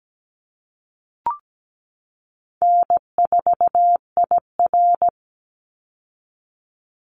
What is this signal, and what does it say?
Morse code, heard as a keyed tone, about 17 wpm, 700 Hz. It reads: N4IR
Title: Easy Call Signs Repeated 2x - 17wpm